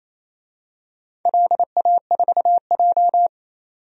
Morse code, heard as a keyed tone, about 28 wpm, 690 Hz. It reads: LA4J
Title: DXCC-100 Call Signs - 4 Characters - 28wpm